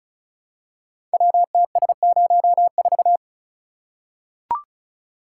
Morse code, heard as a keyed tone, about 36 wpm, 700 Hz. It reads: WTS04
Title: Code Groups - 5 Letter:Numbers - 35wpm